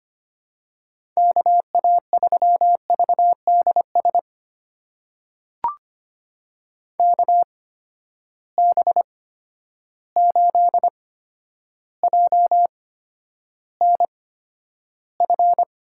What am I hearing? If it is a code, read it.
Morse code, heard as a keyed tone, about 25 wpm, spaced as for 11 wpm, 700 Hz. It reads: KA3VDS KB8JNF
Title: USA Call Signs - 6 Characters - 8x Character Spacing - 25wpm